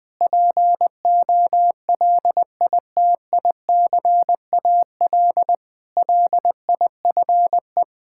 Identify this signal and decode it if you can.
Morse code, heard as a keyed tone, about 20 wpm, 700 Hz. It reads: POLITICAL LIFE
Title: Sentences from Top 1000 Words - Review Part 5 - 20wpm